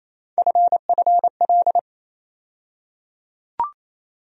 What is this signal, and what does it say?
Morse code, heard as a keyed tone, about 28 wpm, 700 Hz. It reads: FFL
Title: USA Call Sign Postfixes - 3 Letters 28wpm